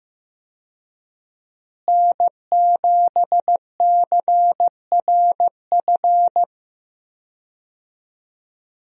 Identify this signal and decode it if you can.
Morse code, heard as a keyed tone, about 15 wpm, 700 Hz. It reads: N7CRF
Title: USA Call Signs - 5 Characters - Repeated 3x - 15wpm